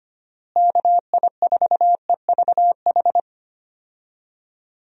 Morse code, heard as a keyed tone, about 25 wpm, 700 Hz. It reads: KI4EVH
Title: USA Call Signs - 6 Characters - Repeated 3x - 25wpm